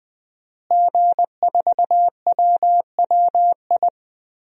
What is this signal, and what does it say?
Morse code, heard as a keyed tone, about 20 wpm, 700 Hz. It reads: G4WWI
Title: DXCC-100 Call Signs - 5 Characters - Repeated 3x - 20wpm